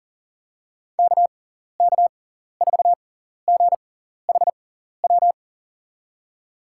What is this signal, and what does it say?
Morse code, heard as a keyed tone, about 37 wpm, spaced as for 14 wpm, 700 Hz. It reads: KK4GHW
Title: USA Call Signs - 6 Characters - 6x Character Spacing - 40wpm